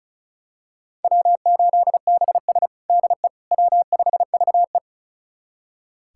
Morse code, heard as a keyed tone, about 36 wpm, 700 Hz. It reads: W8BS DE W5VE
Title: USA Callsign DE Callsign - 4 Characters - 35wpm